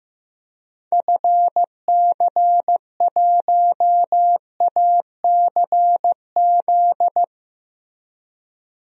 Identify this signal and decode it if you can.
Morse code, heard as a keyed tone, about 15 wpm, 700 Hz. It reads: FC1ACZ
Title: DXCC-100 Call Signs - 6 Characters - Repeated 2x - 15wpm